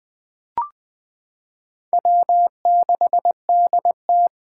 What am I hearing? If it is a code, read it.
Morse code, heard as a keyed tone, about 21 wpm, 700 Hz. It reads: W6DT
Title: 2023 SOTA Chaser Call Signs - 20wpm